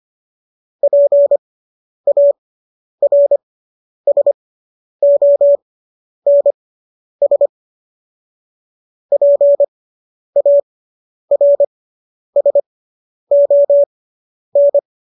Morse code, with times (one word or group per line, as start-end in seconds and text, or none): 0.8-7.5 PARSONS
9.1-14.8 PARSON